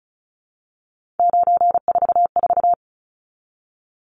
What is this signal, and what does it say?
Morse code, heard as a keyed tone, about 34 wpm, 700 Hz. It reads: 944
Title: Numbers - 3-Digit 35wpm